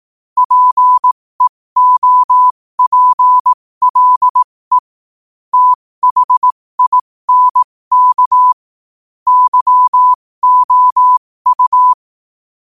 Morse code, heard as a keyed tone, about 18 wpm, spaced as for 15 wpm, 1000 Hz. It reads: PEOPLE THINK YOU
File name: qotdfiles / 15wpm